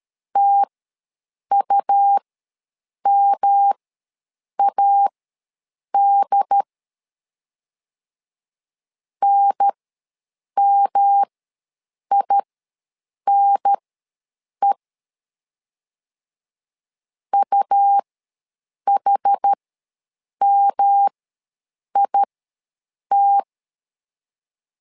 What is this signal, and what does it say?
Morse code, heard as a keyed tone, about 13 wpm, spaced as for 7 wpm, 800 Hz. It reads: TUMAD NMINE UHMIT
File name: LEZIONI / 3 MWRO / 3 M ws 12wpm 1745